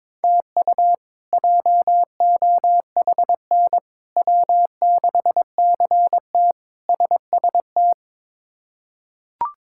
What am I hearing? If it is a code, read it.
Morse code, heard as a keyed tone, about 21 wpm, 700 Hz. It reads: TU JOHN W6CT SST
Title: K1USN SST QSO Practice - 22wpm